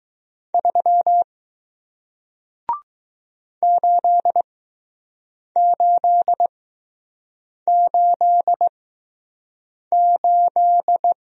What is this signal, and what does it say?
Morse code, heard as a keyed tone, about 20 wpm, spaced as for 6 wpm, 700 Hz. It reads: 3 8888